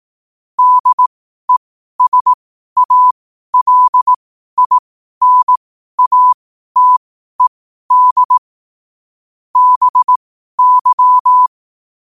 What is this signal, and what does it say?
Morse code, heard as a keyed tone, about 18 wpm, spaced as for 12 wpm, 1000 Hz. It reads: DESALINATED BY
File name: qotdfiles / 12wpm